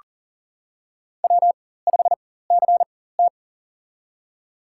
Morse code, heard as a keyed tone, about 40 wpm, spaced as for 19 wpm, 700 Hz. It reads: W5CT
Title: USA Call Signs - 4 Characters - 4x Character Spacing - 40wpm